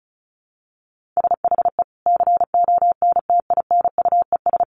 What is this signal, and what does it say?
Morse code, heard as a keyed tone, about 34 wpm, 700 Hz. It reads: SHE CONTINUES
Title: Easy Phrases 35wpm